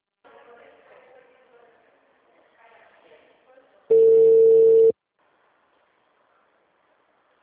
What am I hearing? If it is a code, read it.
Ringback tone (China, 450 Hz), with speech.
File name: out- -2003- - - .6